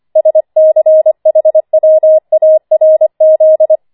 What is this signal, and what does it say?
Morse code, heard as a keyed tone, about 25 wpm, 630 Hz. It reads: SCHWARZ